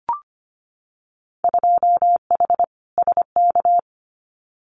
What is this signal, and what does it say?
Morse code, heard as a keyed tone, about 25 wpm, 700 Hz. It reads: 2H SK
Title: Winter Field Day Exchanges Encore 25wpm